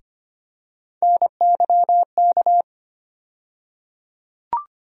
Morse code, heard as a keyed tone, about 25 wpm, 700 Hz. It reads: NYK
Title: USA Call Sign Postfixes - 3 Letters 25wpm